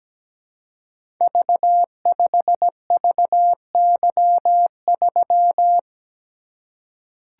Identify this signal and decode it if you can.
Morse code, heard as a keyed tone, about 17 wpm, 700 Hz. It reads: V5VY3
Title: Code Groups - 5 Letter:Numbers - 17wpm